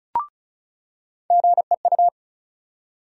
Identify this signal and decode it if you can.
Morse code, heard as a keyed tone, about 34 wpm, 700 Hz. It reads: GEU
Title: USA Call Sign Postfixes - 3 Letters 35wpm